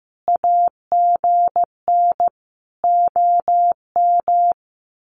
Morse code, heard as a keyed tone, about 15 wpm, 700 Hz. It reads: AGN OM
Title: Common QSO Phrases 15wpm